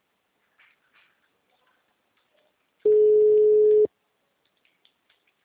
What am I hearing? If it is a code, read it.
Ringback tone (Europe, 425 Hz).